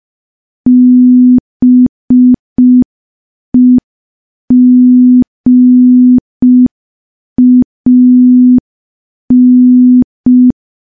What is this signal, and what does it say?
Morse code, heard as a keyed tone, about 5 wpm, 250 Hz. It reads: BEGAN
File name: morse / morse-training-05WPM